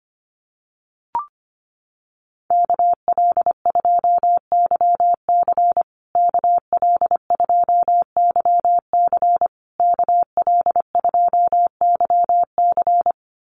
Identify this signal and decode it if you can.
Morse code, heard as a keyed tone, about 25 wpm, 700 Hz. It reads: KL2YC KL2YC KL2YC
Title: USA Call Signs Repeated 3x 25wpm